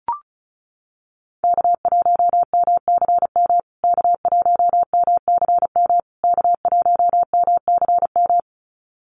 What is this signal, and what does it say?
Morse code, heard as a keyed tone, about 34 wpm, 700 Hz. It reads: K1MCM K1MCM K1MCM
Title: USA Call Signs Repeated 3x 35wpm